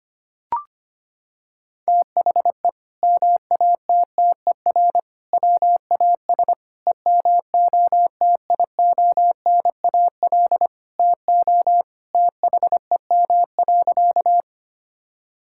Morse code, heard as a keyed tone, about 25 wpm, 700 Hz. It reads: THE MATTER WAS EMOTIONAL TO THEM.
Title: Sentences from Top 1000 Words - Review Part 5 - 25wpm